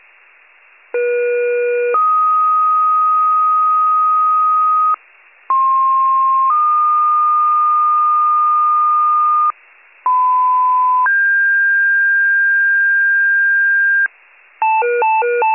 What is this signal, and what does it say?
Two-tone page, heard as a one-second tone, then a three-second tone, about 480 then 1250 Hz, about 1060 then 1220 Hz, about 1010 then 1600 Hz. It is heard from 0.9 to 14.1 s.